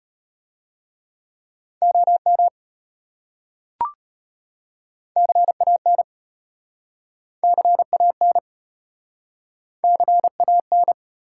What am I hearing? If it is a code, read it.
Morse code, heard as a keyed tone, about 33 wpm, 700 Hz. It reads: OM CAN CAN CAN